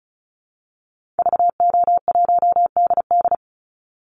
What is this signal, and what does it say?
Morse code, heard as a keyed tone, about 34 wpm, 700 Hz. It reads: VO1DD
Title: Call Signs Repeated 2x - 35wpm